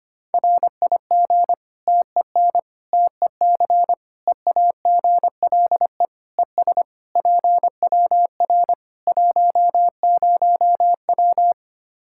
Morse code, heard as a keyed tone, about 25 wpm, 700 Hz. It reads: RIG TEN TEC EAGLE ES PWR 10W
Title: K1USN SST QSO Practice 25wpm